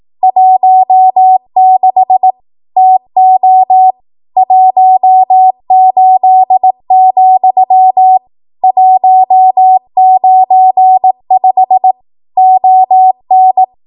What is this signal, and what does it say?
Morse code, heard as a keyed tone, about 18 wpm, 750 Hz. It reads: 16 TO 18, 195 ON
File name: Archive / 18 WPM / 18WPM